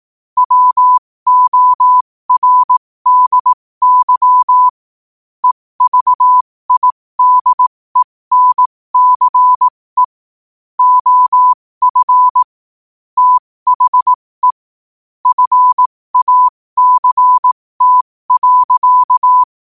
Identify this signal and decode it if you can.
Morse code, heard as a keyed tone, about 18 wpm, spaced as for 15 wpm, 1000 Hz. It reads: WORDY EVIDENCE OF THE FACT.
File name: qotdfiles / 15wpm